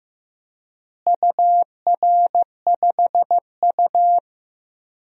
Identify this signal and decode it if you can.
Morse code, heard as a keyed tone, about 15 wpm, 700 Hz. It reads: UR5U
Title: DXCC-100 Call Signs - 4 Characters - Repeated 3x - 15wpm